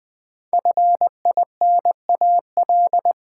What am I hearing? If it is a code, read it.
Morse code, heard as a keyed tone, about 20 wpm, 700 Hz. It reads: FINAL
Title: Sentences from Top 800 Words - New Word Drill - 20wpm